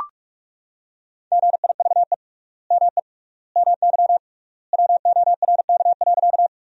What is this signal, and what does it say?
Morse code, heard as a keyed tone, about 41 wpm, 700 Hz. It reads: GIVE ME MY WORK.
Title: 3x Word-Spacing - Sentences from Top 200 Words - 45wpm